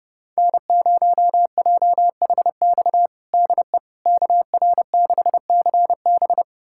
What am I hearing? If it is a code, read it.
Morse code, heard as a keyed tone, about 32 wpm, 700 Hz. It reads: N0JHX DE KR6CB
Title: USA Callsign DE Callsign - 5 Characters - 30wpm